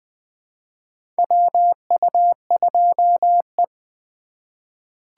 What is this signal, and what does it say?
Morse code, heard as a keyed tone, about 20 wpm, 700 Hz. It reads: WU2E
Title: USA Call Signs - 4 Characters - 2x Character Spacing - 20wpm